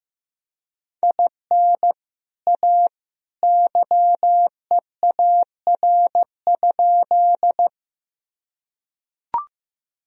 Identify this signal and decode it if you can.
Morse code, heard as a keyed tone, about 15 wpm, 700 Hz. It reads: IN A YEAR?
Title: Sentences from Top 1000 Words - Review Part 1 - 15wpm